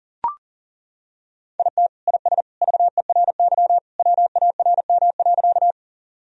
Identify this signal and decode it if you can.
Morse code, heard as a keyed tone, about 37 wpm, 700 Hz. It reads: IT IS VERY WARM.
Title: Sentences from Top 1000 Words - Review Part 2 - 40wpm